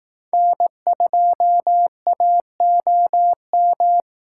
Morse code, heard as a keyed tone, about 18 wpm, 700 Hz. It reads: N2AOM
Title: Speed Racing - USA Call Signs 15wpm